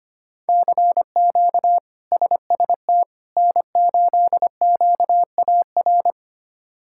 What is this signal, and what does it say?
Morse code, heard as a keyed tone, about 25 wpm, 700 Hz. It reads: CQ SST N8QAR
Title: K1USN SST QSO Practice - 25wpm